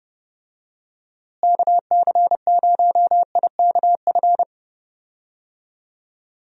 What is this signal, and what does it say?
Morse code, heard as a keyed tone, about 30 wpm, 700 Hz. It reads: KC0IKF
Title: USA Call Signs - 6 Characters - Repeated 3x - 30wpm